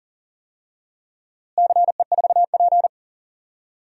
Morse code, heard as a keyed tone, about 40 wpm, 700 Hz. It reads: CE4P
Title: DXCC-100 Call Signs - 4 Characters - Repeated 3x - 40wpm